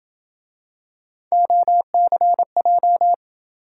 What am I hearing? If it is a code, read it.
Morse code, heard as a keyed tone, about 27 wpm, 700 Hz. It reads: OCJ